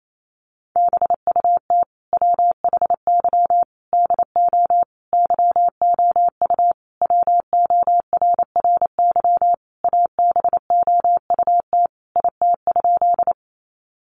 Morse code, heard as a keyed tone, about 28 wpm, 700 Hz. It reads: BUT WHY DO YOU WORRY ABOUT IT?
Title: Sentences from Top 1100 Words 28wpm